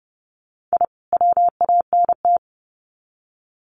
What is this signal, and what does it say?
Morse code, heard as a keyed tone, about 30 wpm, 700 Hz. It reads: I WANT
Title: Sets of 2 Words 30wpm